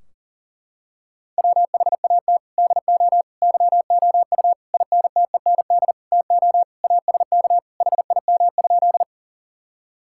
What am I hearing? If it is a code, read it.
Morse code, heard as a keyed tone, about 40 wpm, 700 Hz. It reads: WHAT DO YOU INTEND TO ASK HIM?
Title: Sentences from Top 1600 Words 40wpm